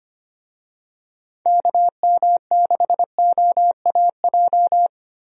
Morse code, heard as a keyed tone, about 25 wpm, 700 Hz. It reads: KM6OAJ
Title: USA Call Signs - 6 Characters - Repeated 3x - 25wpm